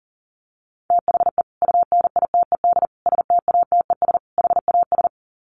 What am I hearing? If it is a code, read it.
Morse code, heard as a keyed tone, about 40 wpm, 700 Hz. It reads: THE UNITED STATES HAS
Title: Sets of 4 Words 40wpm